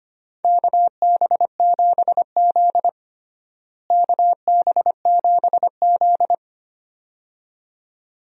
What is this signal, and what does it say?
Morse code, heard as a keyed tone, about 25 wpm, 700 Hz. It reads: KB7Z KB7Z
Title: USA Call Signs - 4 Characters - Repeated 2x - 25wpm